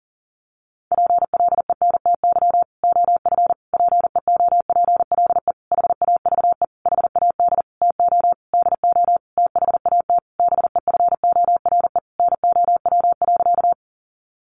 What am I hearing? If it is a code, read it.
Morse code, heard as a keyed tone, about 40 wpm, 700 Hz. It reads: PLENTY OF PEOPLE HAVE HAD TO DO THAT BEFORE NOW.
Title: Sentences from Top 1300 Words 40wpm